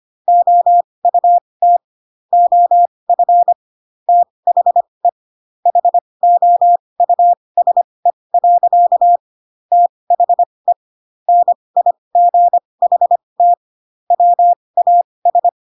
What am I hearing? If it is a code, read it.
Morse code, heard as a keyed tone, about 25 wpm, spaced as for 20 wpm, 700 Hz. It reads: OUT OF THE HOUSE. THE NIGHT WAS
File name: Pinocchio / Pinocchio ch6 15 25 ik1hge